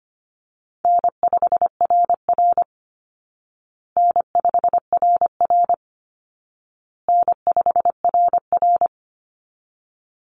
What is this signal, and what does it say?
Morse code, heard as a keyed tone, about 25 wpm, 700 Hz. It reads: N5RR N5RR N5RR
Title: Easy Call Signs Repeated 3x - 25wpm